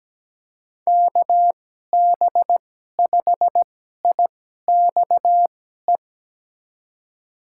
Morse code, heard as a keyed tone, about 17 wpm, spaced as for 12 wpm, 700 Hz. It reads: KB5IXE
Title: USA Call Signs - 6 Characters - 2x Character Spacing - 17wpm